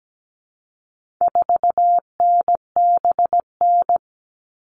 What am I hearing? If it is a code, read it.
Morse code, heard as a keyed tone, about 17 wpm, 700 Hz. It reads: 4NBN
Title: All Cut Numbers - 4-Digit 17wpm